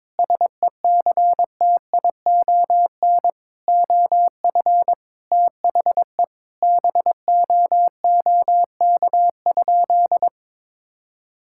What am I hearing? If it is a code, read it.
Morse code, heard as a keyed tone, about 22 wpm, 700 Hz. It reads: SECTION OF THE BOOK?
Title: Sentences from Top 1000 Words - Review Part 5 - 22wpm